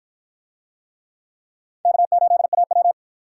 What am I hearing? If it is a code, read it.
Morse code, heard as a keyed tone, about 50 wpm, 700 Hz. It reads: K8AW